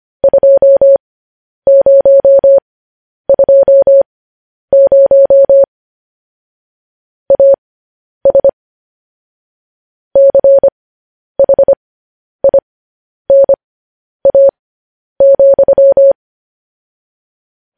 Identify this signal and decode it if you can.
Morse code, heard as a keyed tone, about 25 wpm, spaced as for 10 wpm, 550 Hz. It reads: 2020 AS CHINA,
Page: news headlines in morse code at 10 wpm